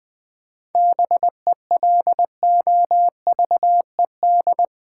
Morse code, heard as a keyed tone, about 21 wpm, 700 Hz. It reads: BELOVED
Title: 7 Letter Words - First 3 Letters Given - 20wpm